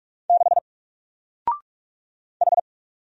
Morse code, heard as a keyed tone, about 46 wpm, 700 Hz. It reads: 6 H
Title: Single Letter-Number Alt 45wpm